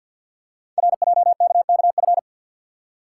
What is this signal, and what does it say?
Morse code, heard as a keyed tone, about 49 wpm, 690 Hz. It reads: RJKKF
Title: Code Groups - 5 Letters - 50wpm